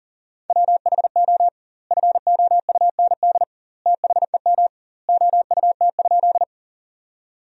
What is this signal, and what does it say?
Morse code, heard as a keyed tone, about 40 wpm, 700 Hz. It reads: WHO FOUND THEM OUT?
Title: 2x Word-Spacing - Sentences from Top 300 Words - 40wpm